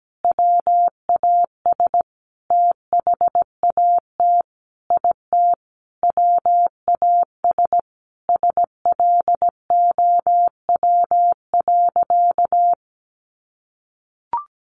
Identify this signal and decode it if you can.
Morse code, heard as a keyed tone, about 17 wpm, 700 Hz. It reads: WAS THAT IT WAS SLOW.
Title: Sentences from Top 1400 Words 17wpm